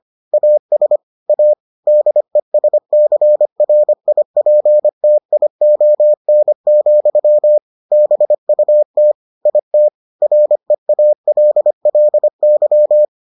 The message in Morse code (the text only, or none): AS A DESCRIPTION, BUT IT REALLY